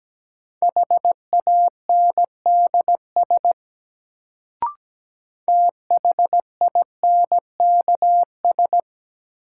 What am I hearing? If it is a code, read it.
Morse code, heard as a keyed tone, about 17 wpm, 700 Hz. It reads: HANDS THINKS
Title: Sentences from Top 1000 Words - Review Part 6 - New Word Drill - 17wpm